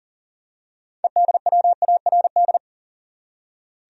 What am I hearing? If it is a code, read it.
Morse code, heard as a keyed tone, about 40 wpm, 700 Hz. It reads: EDWARD
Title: USA Common Names - 40wpm